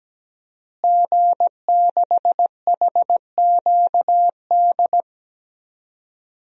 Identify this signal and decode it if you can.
Morse code, heard as a keyed tone, about 17 wpm, 700 Hz. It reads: G6HQD
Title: DXCC-100 Call Signs - 5 Characters - Repeated 3x - 17wpm